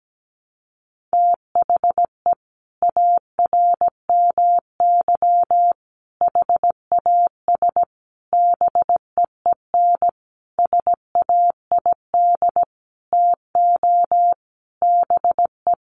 Morse code, heard as a keyed tone, about 17 wpm, 700 Hz. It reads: THE ARMY HAS BEEN SAID TO BE
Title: Sentences from Top 1400 Words 17wpm